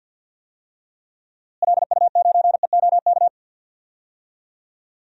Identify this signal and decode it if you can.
Morse code, heard as a keyed tone, about 53 wpm, 700 Hz. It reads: LU9EOK
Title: DXCC-100 Call Signs - 6 Characters - Repeated 2x - 50wpm